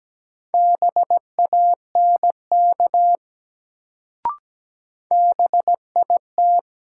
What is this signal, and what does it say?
Morse code, heard as a keyed tone, about 17 wpm, 700 Hz. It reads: BANK BIT
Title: Sentences from Top 900 Words - New Word Drill - 17wpm